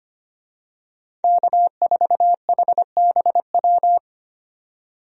Morse code, heard as a keyed tone, about 25 wpm, 700 Hz. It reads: K4HBW
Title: USA Call Signs - 5 Characters - 25wpm